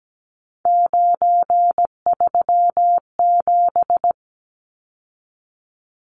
Morse code, heard as a keyed tone, about 17 wpm, 700 Hz. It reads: 937
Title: Numbers - 3-Digit 17wpm